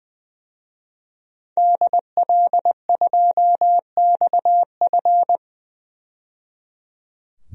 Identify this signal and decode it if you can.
Morse code, heard as a keyed tone, about 20 wpm, 700 Hz. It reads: DL2XF
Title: DXCC-100 Call Signs - 5 Characters - 20wpm